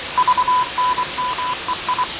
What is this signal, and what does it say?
Morse code, heard as a keyed tone, about 24 wpm, 1000 Hz, with noise. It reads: VNMEI
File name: fav22-6824usb-20251029-1420z-voice-test-win10